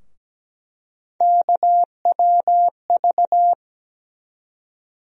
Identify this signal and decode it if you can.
Morse code, heard as a keyed tone, about 17 wpm, 700 Hz. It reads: KWV
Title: USA Call Sign Postfixes - 3 Letters 17wpm